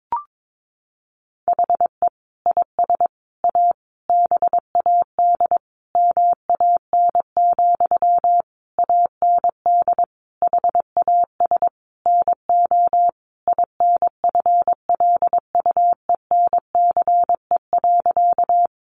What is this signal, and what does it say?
Morse code, heard as a keyed tone, about 22 wpm, 700 Hz. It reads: HE IS A BAD MAN, AND HAS NO INFLUENCE.
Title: Sentences from Top 1100 Words 22wpm